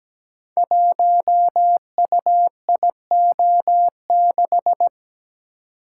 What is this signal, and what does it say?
Morse code, heard as a keyed tone, about 17 wpm, 700 Hz. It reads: 1UIO6
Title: Code Groups - 5 Letter:Numbers - 17wpm